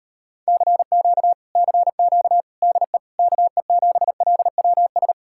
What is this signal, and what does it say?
Morse code, heard as a keyed tone, about 33 wpm, 700 Hz. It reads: CQ CQ DE KE7LWS